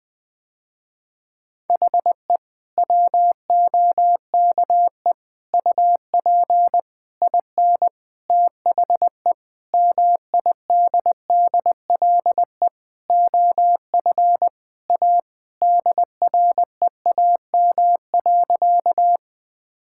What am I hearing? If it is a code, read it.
Morse code, heard as a keyed tone, about 20 wpm, 700 Hz. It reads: HE WOKE UP IN THE MIDDLE OF A DREAM.
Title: Sentences with 4-Word Phrases - Spoken Beginning - 20wpm